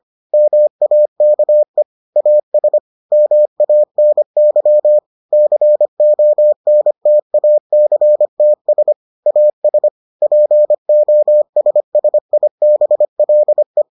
Morse code, heard as a keyed tone, about 25 wpm, 600 Hz. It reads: MAKE AS MANY CONTACTS AS POSSIBLE